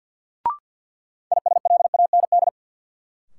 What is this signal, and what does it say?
Morse code, heard as a keyed tone, about 48 wpm, 700 Hz. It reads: ISLAND
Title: Sentences from Top 500 Words - New Word Drill - 50wpm